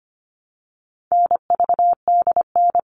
Morse code, heard as a keyed tone, about 25 wpm, 700 Hz. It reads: NVDN
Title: All Cut Numbers - 4-Digit 25wpm